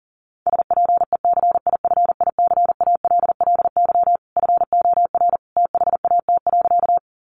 Morse code, heard as a keyed tone, about 40 wpm, 700 Hz. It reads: SPECIFICALLY FOR THAT.
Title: Sentences from Top 1100 Words 40wpm